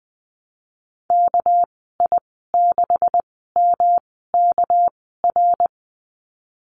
Morse code, heard as a keyed tone, about 19 wpm, spaced as for 14 wpm, 700 Hz. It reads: KI6MKR
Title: 231 Lesson 37g - Letter x - Call Signs 20wpm